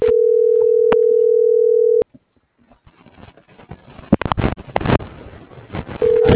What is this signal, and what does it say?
Ringback tone (North America), 440 + 480 Hz.